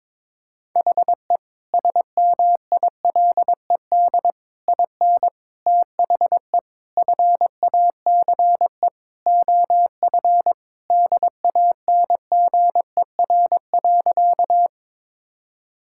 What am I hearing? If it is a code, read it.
Morse code, heard as a keyed tone, about 22 wpm, 700 Hz. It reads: HE SMILED IN THE FACE OF DANGER.
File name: practice / 22wpm / Sentences with 4-Word Phrases - 22wpm